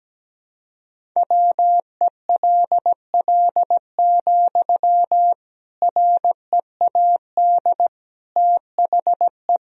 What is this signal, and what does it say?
Morse code, heard as a keyed tone, about 17 wpm, 700 Hz. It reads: WELL, READ THE
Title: Sentences from Top 300 Words and Abbreviations 17wpm